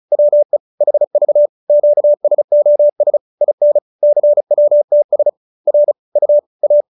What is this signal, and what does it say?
Morse code, heard as a keyed tone, about 34 wpm, 590 Hz. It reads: WE HV QSOS IN CWTS R U A